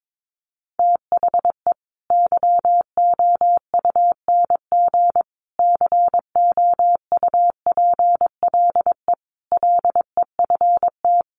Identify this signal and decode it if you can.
Morse code, heard as a keyed tone, about 22 wpm, 700 Hz. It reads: THE YOUNG COUPLE LEFT
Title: Sentences from Top 1100 Words 22wpm